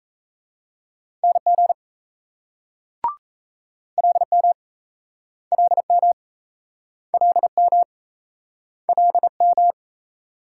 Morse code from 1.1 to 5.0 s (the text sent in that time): NG LM